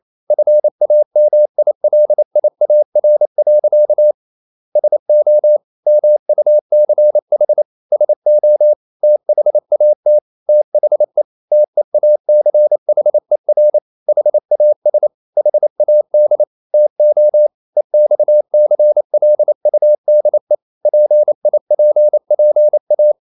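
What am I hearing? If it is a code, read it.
Morse code, heard as a keyed tone, about 28 wpm, 600 Hz. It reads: FAMILIAR. SO MUCH SO THAT THE TEACHER HAS HAD TO EXCLUDE PIPPA
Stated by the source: Morse code practice at 28 wpm, text of Ken Follett's novel Never in English, from CW med Gnister.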